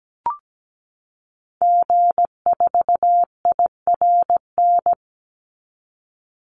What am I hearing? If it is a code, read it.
Morse code, heard as a keyed tone, about 17 wpm, 700 Hz. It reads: G4IRN
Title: Call Signs Repeated 3x - 17wpm